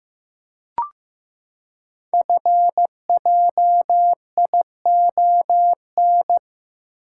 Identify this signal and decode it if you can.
Morse code, heard as a keyed tone, about 15 wpm, 700 Hz. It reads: FJION
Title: Code Groups - 5 Letters - 15wpm